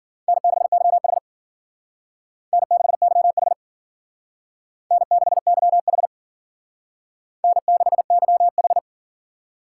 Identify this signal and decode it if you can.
Morse code, heard as a keyed tone, about 51 wpm, 700 Hz. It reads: N6YH N6YH N6YH N6YH